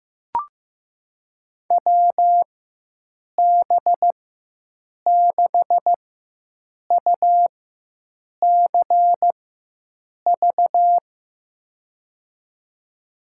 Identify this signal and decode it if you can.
Morse code, heard as a keyed tone, about 15 wpm, spaced as for 7 wpm, 700 Hz. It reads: WB6UCV
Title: USA Call Signs - 6 Characters - 4x Character Spacing - 15wpm